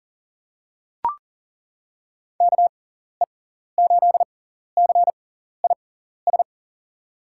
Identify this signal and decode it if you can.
Morse code, heard as a keyed tone, about 40 wpm, spaced as for 14 wpm, 700 Hz. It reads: KE8CIS
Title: USA Call Signs - 6 Characters - 6x Character Spacing - 40wpm